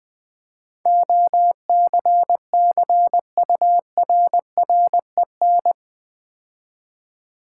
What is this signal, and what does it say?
Morse code, heard as a keyed tone, about 20 wpm, 700 Hz. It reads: OCCURREN
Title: Build A Word - 4 Letters At A Time - 10-Letter Words - 20wpm